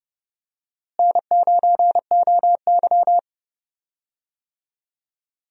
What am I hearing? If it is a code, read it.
Morse code, heard as a keyed tone, about 31 wpm, 700 Hz. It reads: N9OY
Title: USA Call Signs - 4 Characters - Repeated 2x - 30wpm